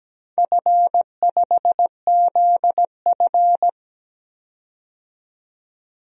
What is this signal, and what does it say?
Morse code, heard as a keyed tone, about 17 wpm, 700 Hz. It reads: F5ZF
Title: DXCC-100 Call Signs - 4 Characters - 17wpm